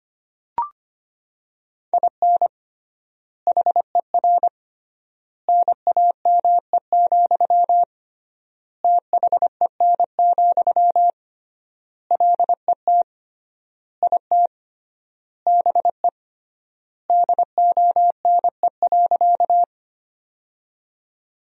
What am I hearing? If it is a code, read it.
Morse code, heard as a keyed tone, about 25 wpm, 700 Hz. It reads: IN HER NAME, THEN, LET IT BE DONE.
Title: 3x Word-Spacing - Sentences from Top 500 Words - 25wpm